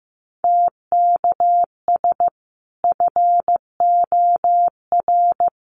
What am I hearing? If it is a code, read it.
Morse code, heard as a keyed tone, about 15 wpm, 700 Hz. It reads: TKS FOR
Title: Common QSO Phrases 15wpm